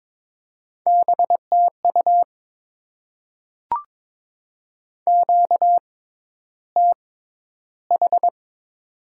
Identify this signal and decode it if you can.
Morse code, heard as a keyed tone, about 22 wpm, spaced as for 11 wpm, 700 Hz. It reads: BTU QTH
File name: Top 20 QSO Elements - 6x Character Spacing - 22wpm